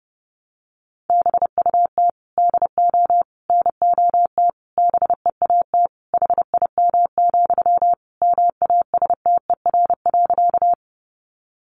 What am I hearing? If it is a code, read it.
Morse code, heard as a keyed tone, about 28 wpm, 700 Hz. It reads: BUT DO NOT BEAT HIM, MASTER.
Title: Sentences from Top 1300 Words 30wpm